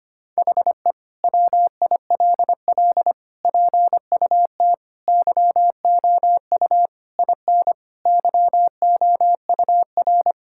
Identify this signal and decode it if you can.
Morse code, heard as a keyed tone, about 25 wpm, 700 Hz. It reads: HE WILL PUT YOU IN YOUR
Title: Sentences from Top 1000 Words - Review Part 5 - 25wpm